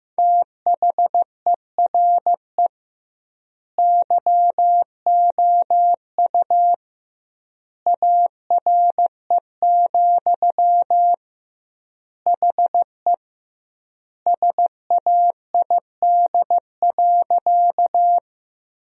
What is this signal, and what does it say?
Morse code, heard as a keyed tone, about 16 wpm, 700 Hz. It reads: THERE YOU ARE, HE SAID.
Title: 2x Word-Spacing - Sentences from Top 100 Words - 15wpm